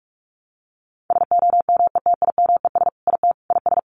Morse code, heard as a keyed tone, about 44 wpm, 700 Hz. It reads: SOMETIMES IT IS
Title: Sentences from Top 600 Words 45wpm